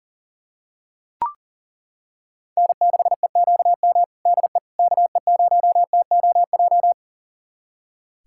Morse code, heard as a keyed tone, about 40 wpm, 700 Hz. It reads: N6EQM DE KE0TOJ
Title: USA Callsign DE Callsign 40wpm